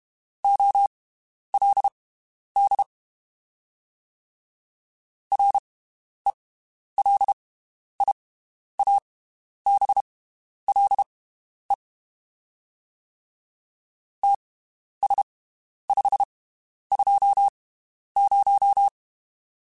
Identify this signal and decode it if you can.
Morse code, heard as a keyed tone, about 32 wpm, spaced as for 11 wpm, 800 Hz. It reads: OLD RELIABLE TS520
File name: (spaced) / QSOs Spaced-mp3s / WD0FBT 1 @30 sp